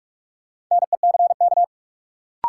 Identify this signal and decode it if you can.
Morse code, heard as a keyed tone, about 46 wpm, 700 Hz. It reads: NECK
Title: Sets of 5 Words Encore - New Word Drill - 45wpm